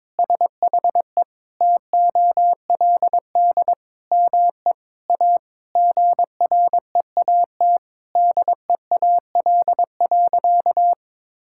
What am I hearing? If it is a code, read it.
Morse code, heard as a keyed tone, about 21 wpm, 700 Hz. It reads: SHE TOLD ME A GREAT DEAL.
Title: Sentences from Top 1000 Words - Review Part 6 - 22wpm